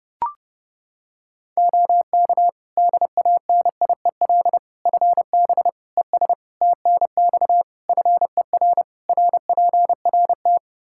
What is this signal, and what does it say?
Morse code, heard as a keyed tone, about 31 wpm, 700 Hz. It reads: OK DANIEL FB ES TNX FER RPRT
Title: Practice QSO - Exchange 2 - Info - 30wpm